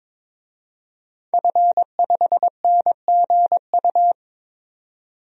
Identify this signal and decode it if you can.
Morse code, heard as a keyed tone, about 22 wpm, 700 Hz. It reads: F5NGU
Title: DXCC-100 Call Signs - 5 Characters - 22wpm